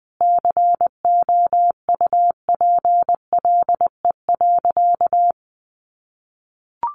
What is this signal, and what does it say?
Morse code, heard as a keyed tone, about 20 wpm, 700 Hz. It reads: COUPLE.
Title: Sentences from Top 1200 Words 20wpm